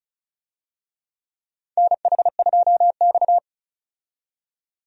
Morse code, heard as a keyed tone, about 36 wpm, 700 Hz. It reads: NH2X
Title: USA Call Signs - 4 Characters - 35wpm